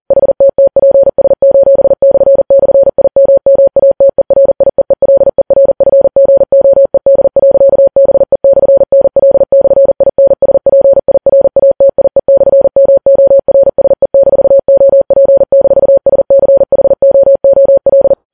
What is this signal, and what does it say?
Morse code, heard as a keyed tone, about 40 wpm, 550 Hz. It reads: HTTPS://IMMATERIEELERFGOED.BE/NL/INSPIRATIE/MORSE-OP-SCHOOL
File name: Files Web tls / Morse on9fv / morse40kps